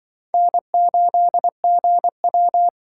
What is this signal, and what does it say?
Morse code, heard as a keyed tone, about 24 wpm, 700 Hz. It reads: N8GW
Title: Speed Racing - USA 1x2 Call Signs 20wpm